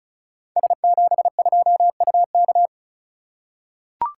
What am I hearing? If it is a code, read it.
Morse code, heard as a keyed tone, about 36 wpm, 690 Hz. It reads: S72UK
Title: Code Groups - 5 Letter:Numbers - 35wpm